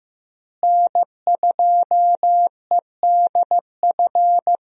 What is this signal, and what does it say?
Morse code, heard as a keyed tone, about 15 wpm, 700 Hz. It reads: N2EDF
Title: USA Call Signs - 5 Characters - Repeated 3x - 15wpm